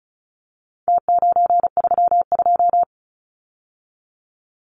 Morse code, heard as a keyed tone, about 34 wpm, 700 Hz. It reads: T932
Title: Numbers - Contest Style - 4-Digit 35wpm